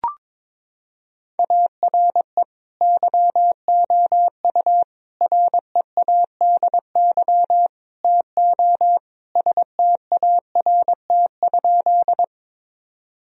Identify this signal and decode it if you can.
Morse code, heard as a keyed tone, about 22 wpm, 700 Hz. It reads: ARE YOU READY TO START?
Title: Sentences from Top 1000 Words - Review Part 1 - 22wpm